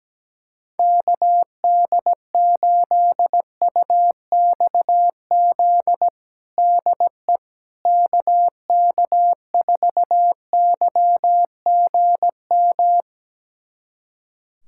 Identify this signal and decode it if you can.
Morse code, heard as a keyed tone, about 17 wpm, 700 Hz. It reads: KD8UXZ DE KK4YGM
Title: USA Callsign DE Callsign - 6 Characters - 17wpm